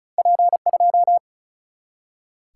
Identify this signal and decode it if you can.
Morse code, heard as a keyed tone, about 36 wpm, 700 Hz. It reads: P 2
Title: Single-Letter Single-Number 35wpm